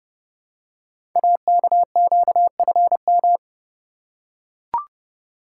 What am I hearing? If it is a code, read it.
Morse code, heard as a keyed tone, about 28 wpm, 700 Hz. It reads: AKQFM
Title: Code Groups - 5 Letter:Numbers - 30wpm